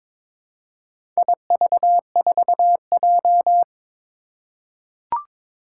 Morse code, heard as a keyed tone, about 22 wpm, 700 Hz. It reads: IV4J
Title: Four Challenging Letter-Number Combinations 22wpm